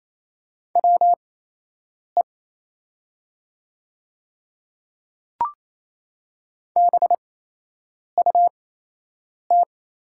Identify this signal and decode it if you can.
Morse code, heard as a keyed tone, about 28 wpm, spaced as for 8 wpm, 700 Hz. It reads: WE BUT
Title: 8x Character Spacing - Top 500 Words - 28wpm